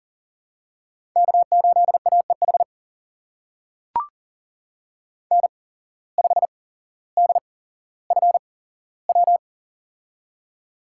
Morse code, heard as a keyed tone, about 39 wpm, spaced as for 17 wpm, 700 Hz. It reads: K8AEH N5DFW
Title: USA Call Signs - 5 Characters - 8x Character Spacing - 40wpm